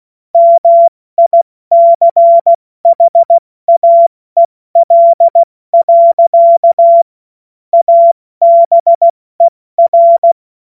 Morse code, heard as a keyed tone, about 16 wpm, 680 Hz. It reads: MICHAEL. ABER